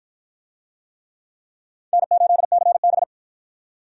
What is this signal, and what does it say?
Morse code, heard as a keyed tone, about 45 wpm, 700 Hz. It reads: N8CB